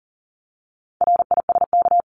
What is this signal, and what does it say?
Morse code, heard as a keyed tone, about 40 wpm, 700 Hz. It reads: RISK
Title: Top 501-750 Words 40wpm